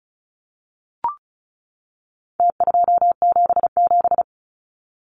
Morse code, heard as a keyed tone, about 34 wpm, 700 Hz. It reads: T277
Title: Numbers - Contest Style - 4-Digit 35wpm